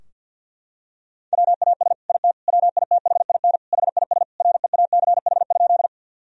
Morse code, heard as a keyed tone, about 50 wpm, 700 Hz. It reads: WAS IT WITHIN HIS REACH?
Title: Sentences from Top 1000 Words - Review Part 4 - 50wpm